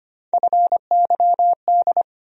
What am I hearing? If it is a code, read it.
Morse code, heard as a keyed tone, about 25 wpm, 700 Hz. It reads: FYD
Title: USA Call Sign Postfixes - 3 Letters 25wpm